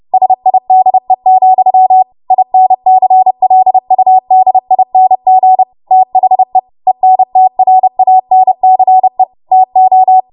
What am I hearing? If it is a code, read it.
Morse code, heard as a keyed tone, about 30 wpm, 750 Hz. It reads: SIDE, INCLUDING THE ENTRANCE TO
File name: Morse / 30WPM